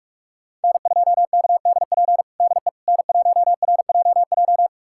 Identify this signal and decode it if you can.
Morse code, heard as a keyed tone, about 42 wpm, 700 Hz. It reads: N2KDP DE N1RJJ
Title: USA Callsign DE Callsign - 5 Characters - 45wpm